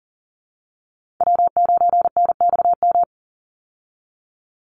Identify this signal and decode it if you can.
Morse code, heard as a keyed tone, about 40 wpm, 700 Hz. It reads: W9NXM
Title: Call Signs Repeated 3x - 40wpm